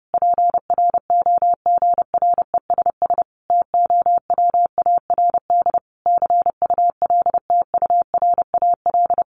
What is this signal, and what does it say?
Morse code, heard as a keyed tone, about 30 wpm, 700 Hz. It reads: PROGRESS TOWARD CULTURAL
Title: Sentences from Top 1500 Words 30wpm